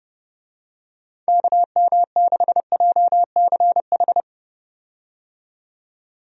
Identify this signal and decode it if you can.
Morse code, heard as a keyed tone, about 30 wpm, 700 Hz. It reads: KM6JCH
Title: USA Call Signs - 6 Characters - 30wpm